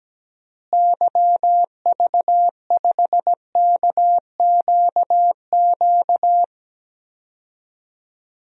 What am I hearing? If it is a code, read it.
Morse code, heard as a keyed tone, about 17 wpm, 700 Hz. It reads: YV5KQQ
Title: DXCC-100 Call Signs - 6 Characters - Repeated 3x - 17wpm